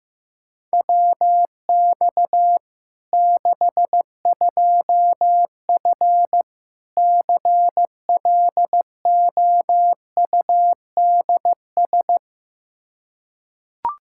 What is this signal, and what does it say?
Morse code, heard as a keyed tone, about 16 wpm, 700 Hz. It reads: WX 62F CLOUDS
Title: K1USN SST QSO Practice 15wpm